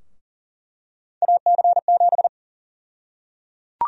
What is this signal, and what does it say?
Morse code, heard as a keyed tone, about 40 wpm, 700 Hz. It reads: AC7
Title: USA Call Sign Prefixes - 40wpm